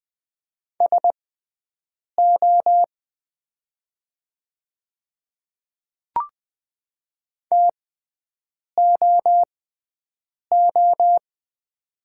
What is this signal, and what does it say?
Morse code, heard as a keyed tone, about 20 wpm, spaced as for 7 wpm, 700 Hz. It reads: SO TOO
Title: 6x Character Spacing - Top 200 Words - 20wpm